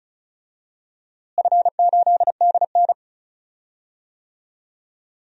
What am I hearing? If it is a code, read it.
Morse code, heard as a keyed tone, about 36 wpm, 700 Hz. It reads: F8DN
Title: DXCC-100 Call Signs - 4 Characters - Repeated 2x - 35wpm